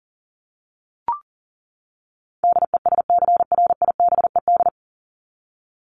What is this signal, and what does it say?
Morse code, heard as a keyed tone, about 40 wpm, 700 Hz. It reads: DESCRIBED
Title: ED Suffix - Top 100 Words with Base Words 40wpm